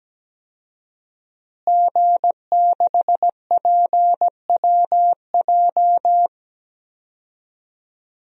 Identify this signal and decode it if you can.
Morse code, heard as a keyed tone, about 17 wpm, 700 Hz. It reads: G6PWJ
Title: DXCC-100 Call Signs - 5 Characters - Repeated 3x - 17wpm